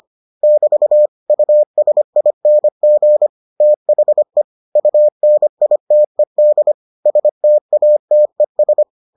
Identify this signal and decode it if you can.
Morse code, heard as a keyed tone, about 25 wpm, 600 Hz. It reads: <BT> USING THE UNITED STATES